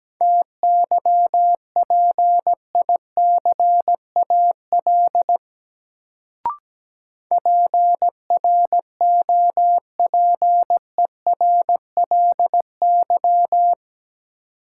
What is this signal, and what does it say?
Morse code, heard as a keyed tone, about 17 wpm, 700 Hz. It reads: TYPICAL PROPERLY
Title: Sentences from Top 1100 Words - New Word Drill - 17wpm